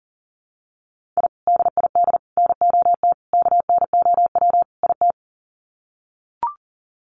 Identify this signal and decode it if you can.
Morse code, heard as a keyed tone, about 40 wpm, 700 Hz. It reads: I DID NOT KNOW IT
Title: Sets of 5 Words Encore2 40wpm